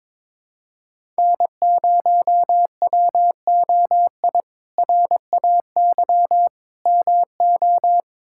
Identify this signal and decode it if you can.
Morse code, heard as a keyed tone, about 22 wpm, 700 Hz. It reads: N0WOI RAY MO
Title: K1USN SST QSO Practice - 22wpm